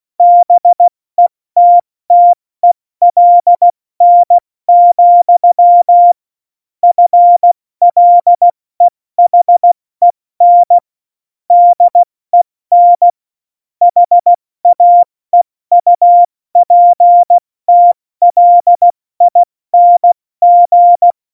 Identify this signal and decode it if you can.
Morse code, heard as a keyed tone, about 16 wpm, 700 Hz. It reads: BETTELN, FLEHEN DEN HAEUPTLING
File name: Morsekurs / Witze / Witz 12. Januar 2021, 70 F80